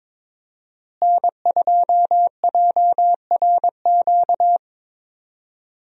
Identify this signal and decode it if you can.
Morse code, heard as a keyed tone, about 21 wpm, 700 Hz. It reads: N2JRQ
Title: USA Call Signs - 5 Characters - Repeated 3x - 22wpm